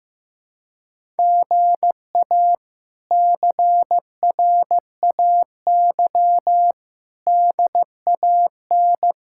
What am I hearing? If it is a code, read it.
Morse code, heard as a keyed tone, about 15 wpm, 700 Hz. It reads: GA CRAY DAN
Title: K1USN SST QSO Practice - 15wpm